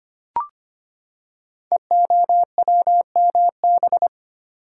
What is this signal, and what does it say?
Morse code, heard as a keyed tone, about 25 wpm, 700 Hz. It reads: EOWMB
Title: Code Groups - 5 Letter:Numbers - 25wpm